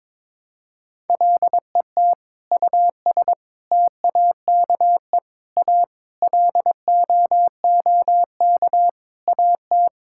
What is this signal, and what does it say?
Morse code, heard as a keyed tone, about 22 wpm, 700 Hz. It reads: LET US TAKE A LOOK AT
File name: practice / 22wpm / Sentences with 8-Word Phrases - 22wpm